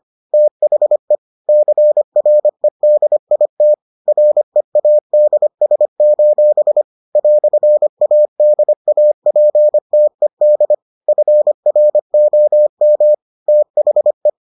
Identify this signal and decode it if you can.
Morse code, heard as a keyed tone, about 25 wpm, 600 Hz. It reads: THE CREDIT READS: "ADAPTED FROM THE